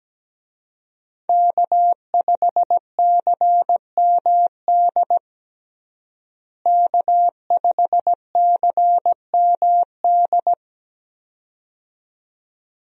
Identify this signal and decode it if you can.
Morse code, heard as a keyed tone, about 17 wpm, 700 Hz. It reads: K5CMD K5CMD
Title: USA Call Signs - 5 Characters - Repeated 2x - 17wpm